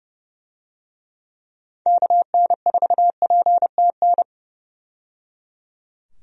Morse code, heard as a keyed tone, about 30 wpm, 700 Hz. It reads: KN4PTN
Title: USA Call Signs - 6 Characters - 30wpm